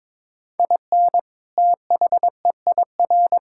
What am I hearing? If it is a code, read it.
Morse code, heard as a keyed tone, about 22 wpm, 700 Hz. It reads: IN THEIR
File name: practice / 22wpm / Sentences with 2-Word Phrases - 22wpm